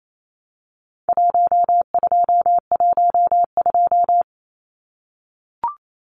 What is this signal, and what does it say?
Morse code, heard as a keyed tone, about 28 wpm, 700 Hz. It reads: 1212
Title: Numbers - Contest Style - 4-Digit 28wpm